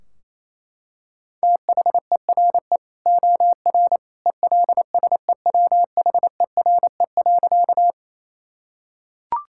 Morse code, heard as a keyed tone, about 27 wpm, 700 Hz. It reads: THERE OR ELSEWHERE.
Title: Sentences from Top 1600 Words 28wpm